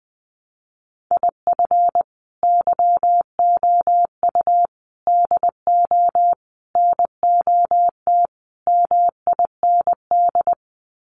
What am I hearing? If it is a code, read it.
Morse code, heard as a keyed tone, about 20 wpm, 700 Hz. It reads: IF YOU DO NOT MIND
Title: Sets of 5 Words 20wpm